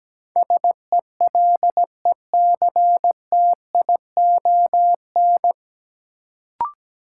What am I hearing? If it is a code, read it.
Morse code, heard as a keyed tone, about 17 wpm, 700 Hz. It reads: SELECTION
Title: Sentences from Top 1400 Words - New Word Drill - 17wpm